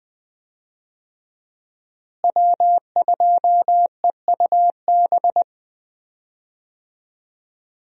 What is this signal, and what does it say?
Morse code, heard as a keyed tone, about 20 wpm, 700 Hz. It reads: W2EUB
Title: USA Call Signs - 5 Characters - Repeated 3x - 20wpm